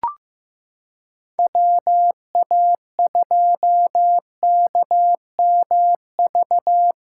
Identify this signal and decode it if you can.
Morse code, heard as a keyed tone, about 16 wpm, 700 Hz. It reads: WA2KMV
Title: USA Callsign DE Callsign 15wpm